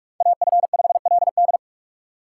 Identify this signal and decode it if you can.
Morse code, heard as a keyed tone, about 44 wpm, 700 Hz. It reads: AF5LD
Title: USA Call Signs - 5 Characters - 4x Character Spacing - 45wpm